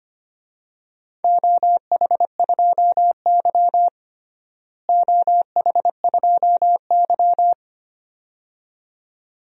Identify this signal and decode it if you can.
Morse code, heard as a keyed tone, about 25 wpm, 700 Hz. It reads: OH2Y OH2Y
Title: DXCC-100 Call Signs - 4 Characters - Repeated 2x - 25wpm